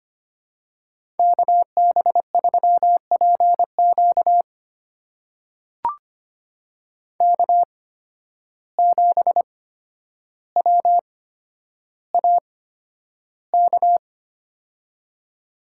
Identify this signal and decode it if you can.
Morse code, heard as a keyed tone, about 25 wpm, spaced as for 11 wpm, 700 Hz. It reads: KB3PQ K7WAK
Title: USA Call Signs - 5 Characters - 8x Character Spacing - 25wpm